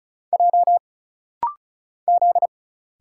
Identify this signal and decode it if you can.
Morse code, heard as a keyed tone, about 36 wpm, 700 Hz. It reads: J Z
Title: Single Letters - Rapid-Fire 35wpm